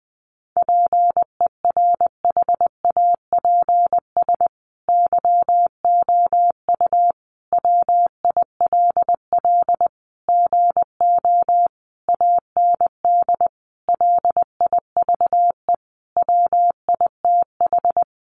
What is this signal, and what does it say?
Morse code, heard as a keyed tone, about 20 wpm, 700 Hz. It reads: PERHAPS YOU WILL GO AND LIVE WITH
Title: Sentences from Top 500 Words 20wpm